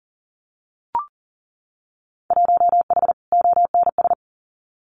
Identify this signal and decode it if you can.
Morse code, heard as a keyed tone, about 42 wpm, 700 Hz. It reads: 1H ONS
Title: Winter Field Day Exchanges Encore 40wpm